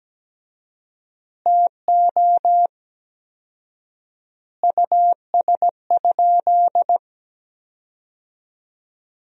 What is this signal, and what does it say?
Morse code, heard as a keyed tone, about 17 wpm, 700 Hz. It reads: TO US?
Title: 4x Word-Spacing - Sentences from Top 400 Words - 17wpm